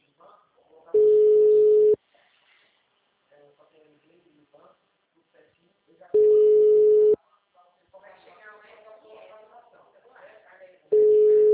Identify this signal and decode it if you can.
Ringback tone (Europe, 425 Hz), with speech.
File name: out- -2017- - - .132